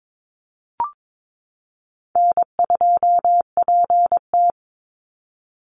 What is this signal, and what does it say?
Morse code, heard as a keyed tone, about 22 wpm, 700 Hz. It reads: N2PT
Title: USA Call Signs 22wpm